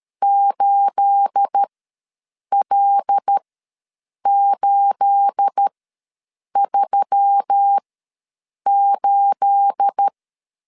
Morse code, heard as a keyed tone, about 13 wpm, spaced as for 7 wpm, 800 Hz. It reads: 8L838
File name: LEZIONI / 9 28 ws 12wpm 3354